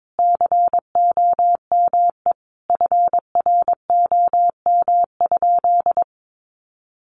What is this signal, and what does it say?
Morse code, heard as a keyed tone, about 22 wpm, 700 Hz. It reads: COME FROM?
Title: Sentences from Top 200 Words 22wpm